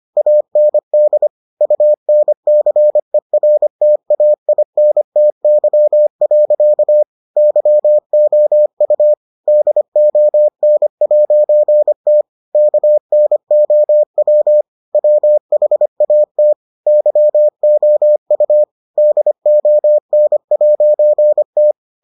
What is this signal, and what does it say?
Morse code, heard as a keyed tone, about 25 wpm, 600 Hz. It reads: AND UNCERTAINTY. YOU DON'T KNOW WHAT YOU DON'T